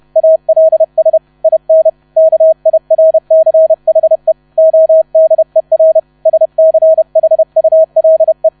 Morse code, heard as a keyed tone, about 31 wpm, 640 Hz. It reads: ALS IN KIRCHE ODER SCHULE
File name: 30wpm